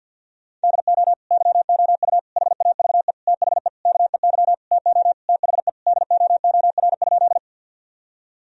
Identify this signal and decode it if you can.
Morse code, heard as a keyed tone, about 50 wpm, 700 Hz. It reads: DO YOU HAVE THE KEY TO THE DOOR?
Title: Sentences from Top 1000 Words - Review Part 4 - 50wpm